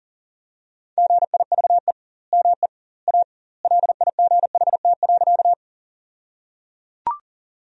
Morse code, heard as a keyed tone, about 37 wpm, 700 Hz. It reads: GIVE ME A LIGHT.
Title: 2x Word-Spacing - Sentences from Top 200 Words - 40wpm